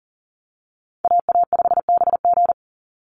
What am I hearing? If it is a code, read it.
Morse code, heard as a keyed tone, about 40 wpm, 700 Hz. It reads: AA5BG
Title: Call Signs Repeated 3x - 40wpm